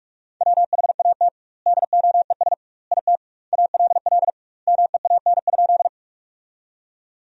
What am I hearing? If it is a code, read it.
Morse code, heard as a keyed tone, about 41 wpm, 700 Hz. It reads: WHAT DOES IT ALL MEAN?
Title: 2x Word-Spacing - Sentences from Top 200 Words - 45wpm